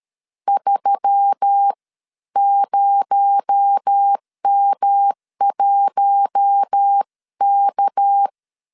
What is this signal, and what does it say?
Morse code, heard as a keyed tone, about 13 wpm, spaced as for 11 wpm, 800 Hz. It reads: 30M1K
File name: CW / DET / 1 6 1500C S A 12wpm 3440